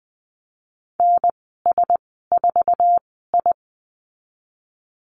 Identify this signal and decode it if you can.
Morse code, heard as a keyed tone, about 19 wpm, spaced as for 14 wpm, 700 Hz. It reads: NS4I
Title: 048 Lesson 9g - Number 4 - Call Signs 20wpm